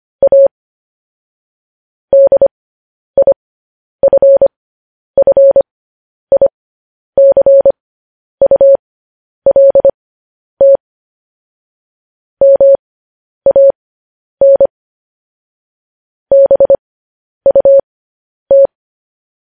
Morse code, heard as a keyed tone, about 25 wpm, spaced as for 10 wpm, 550 Hz. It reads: A DIFFICULT MAN BUT
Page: news headlines in morse code at 10 wpm